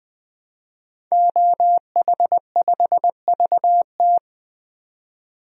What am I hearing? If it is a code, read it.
Morse code, heard as a keyed tone, about 20 wpm, 700 Hz. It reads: OH5VT
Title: DXCC-100 Call Signs - 5 Characters - 20wpm